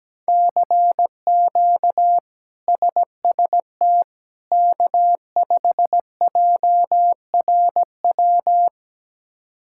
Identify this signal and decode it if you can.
Morse code, heard as a keyed tone, about 17 wpm, 700 Hz. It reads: CQ SST K5JRW
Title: K1USN SST QSO Practice - 17wpm